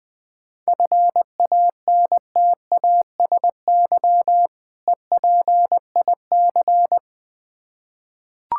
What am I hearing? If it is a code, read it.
Morse code, heard as a keyed tone, about 20 wpm, 700 Hz. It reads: FANTASY EPIC
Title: Difficult Words with Hints - 20wpm